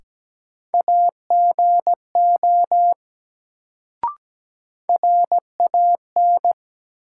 Morse code, heard as a keyed tone, about 17 wpm, 700 Hz. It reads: AGO RAN
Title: Sentences from Top 500 Words - New Word Drill - 17wpm